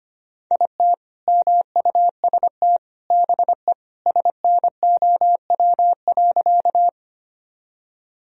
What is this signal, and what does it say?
Morse code, heard as a keyed tone, about 25 wpm, 700 Hz. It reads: IT MUST BE SNOW.
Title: Sentences from Top 1000 Words - Review Part 3 - 25wpm